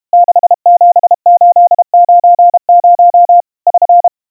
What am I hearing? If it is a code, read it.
Morse code, heard as a keyed tone, about 32 wpm, 700 Hz. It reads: 67890 <SN>
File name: 110 cpm 700 Hz Crispy